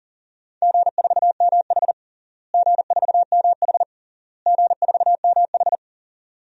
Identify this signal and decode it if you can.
Morse code, heard as a keyed tone, about 40 wpm, 700 Hz. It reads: G4MH G4MH G4MH
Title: DXCC-100 Call Signs - 4 Characters - Repeated 3x - 40wpm